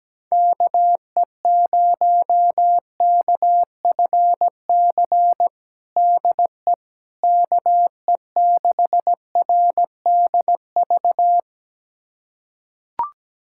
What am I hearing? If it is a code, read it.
Morse code, heard as a keyed tone, about 17 wpm, 700 Hz. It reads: KE0KFC DE KE6RDV
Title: USA Callsign DE Callsign - 6 Characters - 17wpm